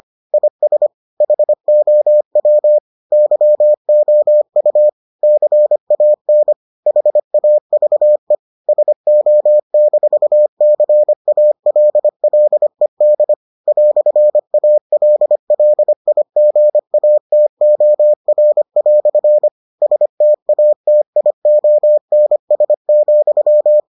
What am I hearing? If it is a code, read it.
Morse code, heard as a keyed tone, about 25 wpm, 600 Hz. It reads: IS HOW YOU CAN HAVE SO-CALLED "ALLIGATOR" STATIONS,